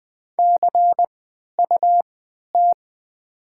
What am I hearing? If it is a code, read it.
Morse code, heard as a keyed tone, about 20 wpm, spaced as for 11 wpm, 700 Hz. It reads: CUT
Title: 3x Character Spacing - Top 300 Words - 20wpm